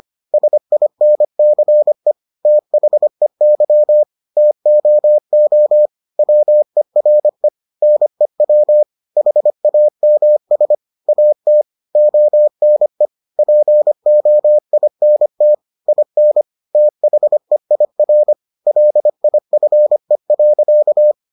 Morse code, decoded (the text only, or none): SINCE THEY TOO WERE NEW HAMS AT ONE POINT IN THEIR LIFE.